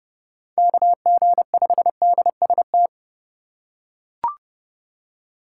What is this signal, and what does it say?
Morse code, heard as a keyed tone, about 29 wpm, 700 Hz. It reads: KG5DST